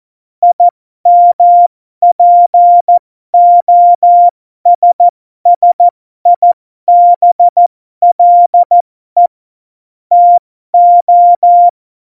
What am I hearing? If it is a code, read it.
Morse code, heard as a keyed tone, about 14 wpm, spaced as for 12 wpm, 700 Hz. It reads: IMPOSSIBLE TO
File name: Morsekurs / Witze